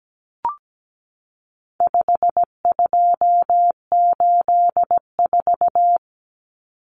Morse code, heard as a keyed tone, about 17 wpm, 700 Hz. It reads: 5284
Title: Common Cut Numbers - 4-Digit 17wpm